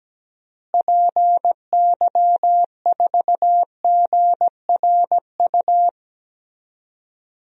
Morse code, heard as a keyed tone, about 17 wpm, 700 Hz. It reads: PY4GRU
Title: DXCC-100 Call Signs - 6 Characters - Repeated 2x - 17wpm